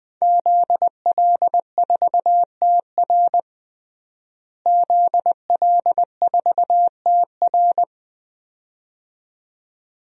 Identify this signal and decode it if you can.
Morse code, heard as a keyed tone, about 20 wpm, 700 Hz. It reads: ZL4TR ZL4TR
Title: DXCC-100 Call Signs - 5 Characters - Repeated 2x - 20wpm